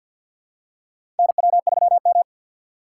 Morse code, heard as a keyed tone, about 48 wpm, 700 Hz. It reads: NW3M
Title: USA Call Signs - 3x Character Spacing - 50wpm